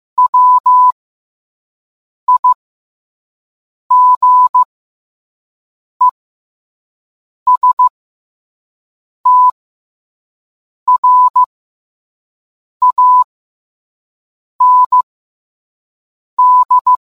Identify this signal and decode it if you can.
Morse code, heard as a keyed tone, about 15 wpm, spaced as for 5 wpm, 1000 Hz. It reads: WIGESTRAND